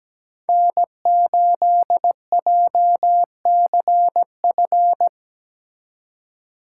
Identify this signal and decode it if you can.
Morse code, heard as a keyed tone, about 20 wpm, 700 Hz. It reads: N8JCF
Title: Speed Racing - USA 1x3 Call Signs 17wpm